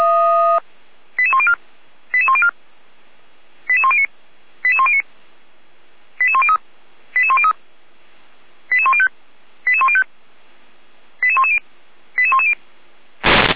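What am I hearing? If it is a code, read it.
Five-tone selective call (ZVEI), heard as quick runs of five tones, tones of about 70 ms. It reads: 88104 88104 88109 88109 88103 88103 88105 88105 88110 88110
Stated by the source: ZVEI five-tone selective call received on SDR, from feuerwehr-orlamuende.org (fire brigade Orlamünde).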